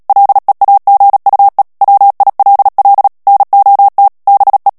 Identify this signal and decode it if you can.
Morse code, heard as a keyed tone, about 37 wpm, 800 Hz. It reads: LEAGUE WILL NOT BE
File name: morse mp3 / ANGLAIS texte clair 3 vitesse 35 mots minute